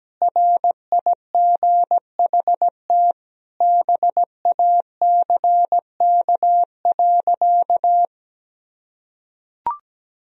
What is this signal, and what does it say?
Morse code, heard as a keyed tone, about 17 wpm, 700 Hz. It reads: RIGHT BACK.
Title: Sentences from Top 1000 Words - Review Part 4 - 17wpm